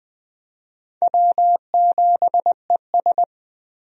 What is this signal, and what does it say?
Morse code, heard as a keyed tone, about 20 wpm, 700 Hz. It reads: W7ES
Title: USA Call Signs - 4 Characters - Repeated 2x - 20wpm